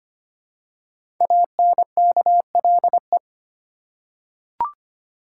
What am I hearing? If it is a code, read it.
Morse code, heard as a keyed tone, about 25 wpm, 700 Hz. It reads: ANKLE
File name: 5 Letter Words - First 3 Letters Given - 25wpm